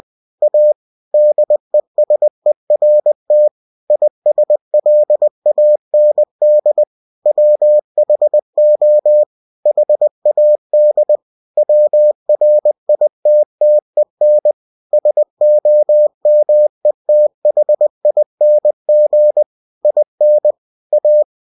Morse code, heard as a keyed tone, about 20 wpm, 600 Hz. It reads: A DESERT ISLAND WHO HAD WRITTEN SOMETHING IN A
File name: cw-books / winnie-the-pooh / winnie-20wpm-page-0092